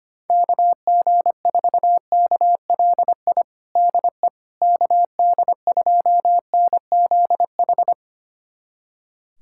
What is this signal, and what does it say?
Morse code, heard as a keyed tone, about 25 wpm, 700 Hz. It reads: KG4KLI DE KD2NZH
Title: USA Callsign DE Callsign 25wpm